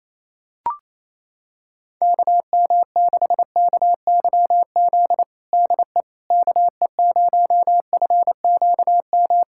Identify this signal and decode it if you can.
Morse code, heard as a keyed tone, about 28 wpm, 700 Hz. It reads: KM6KYZ DE KE0FQM
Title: USA Callsign DE Callsign 28wpm